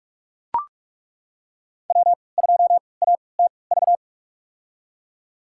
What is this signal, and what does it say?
Morse code, heard as a keyed tone, about 41 wpm, spaced as for 25 wpm, 700 Hz. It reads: W2ATV
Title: USA Call Signs - 5 Characters - 3x Character Spacing - 45wpm